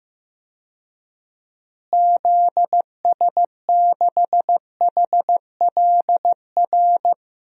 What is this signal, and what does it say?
Morse code, heard as a keyed tone, about 15 wpm, 700 Hz. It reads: ZS6HLR
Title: DXCC-100 Call Signs - 6 Characters - Repeated 3x - 15wpm